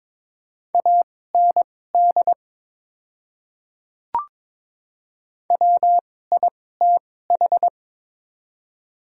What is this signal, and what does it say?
Morse code, heard as a keyed tone, about 22 wpm, spaced as for 16 wpm, 700 Hz. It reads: AND WITH
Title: 2x Character Spacing - Top 100 Words - 22wpm